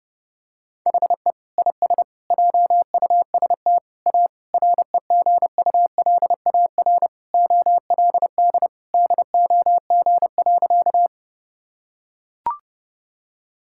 Morse code, heard as a keyed tone, about 30 wpm, 700 Hz. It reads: HE IS JUST A REGULAR OLD DOG.
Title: Sentences from Top 1000 Words - Review Part 6 - 30wpm